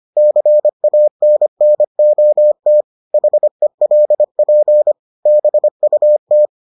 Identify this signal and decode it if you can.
Morse code, heard as a keyed tone, about 25 wpm, 600 Hz. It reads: CANNOT HELP BUT